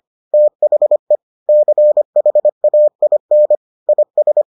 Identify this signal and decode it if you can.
Morse code, heard as a keyed tone, about 25 wpm, 600 Hz. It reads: THE CHAIN IS